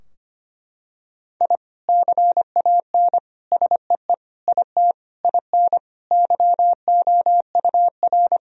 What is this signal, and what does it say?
Morse code, heard as a keyed tone, about 25 wpm, 700 Hz. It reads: I CAN SEE IT IN YOUR
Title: Sentences from Top 1000 Words - Review Part 2 - 25wpm